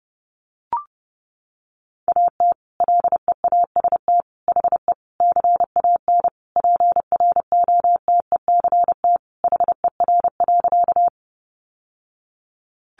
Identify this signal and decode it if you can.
Morse code, heard as a keyed tone, about 30 wpm, 700 Hz. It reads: AT LEAST HE CAN PROTECT HER.
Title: Sentences from Top 900 Words 30wpm